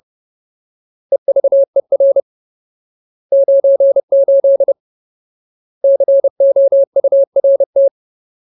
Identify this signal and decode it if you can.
Morse code, heard as a keyed tone, about 30 wpm, 550 Hz. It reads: EVER 98 COURT
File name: Ultimate Word List / 30WPM Ultimate